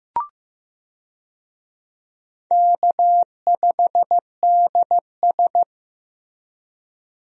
Morse code, heard as a keyed tone, about 15 wpm, 700 Hz. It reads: K5DS
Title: USA Call Signs - 4 Characters - Repeated 3x - 15wpm